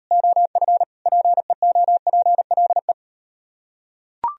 Morse code, heard as a keyed tone, about 38 wpm, 700 Hz. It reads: OF PEOPLE